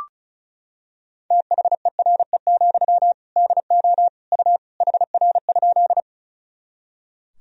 Morse code, heard as a keyed tone, about 35 wpm, 700 Hz. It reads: THERE, DO U HR?
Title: Sentences from Top 300 Words and Abbreviations 35wpm